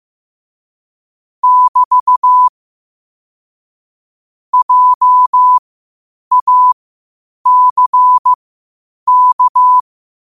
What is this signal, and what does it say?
Morse code, heard as a keyed tone, about 15 wpm, spaced as for 8 wpm, 1000 Hz. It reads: <BT> JACK
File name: qotdfiles / 8wpm